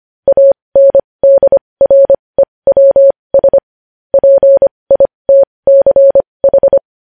Morse code, heard as a keyed tone, about 25 wpm, spaced as for 20 wpm, 550 Hz. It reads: ANDREWS PITCH
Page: News Headlines in Morse Code at 20 WPM